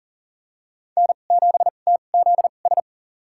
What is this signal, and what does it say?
Morse code, heard as a keyed tone, about 40 wpm, spaced as for 29 wpm, 700 Hz. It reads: N7TZS
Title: USA Call Signs - 5 Characters - 2x Character Spacing - 40wpm